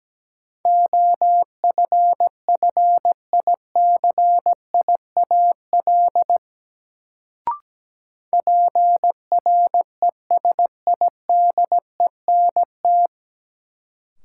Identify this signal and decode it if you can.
Morse code, heard as a keyed tone, about 17 wpm, 700 Hz. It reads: OFFICIAL PRESIDENT
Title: Sentences from Top 1000 Words - New Word Drill - 17wpm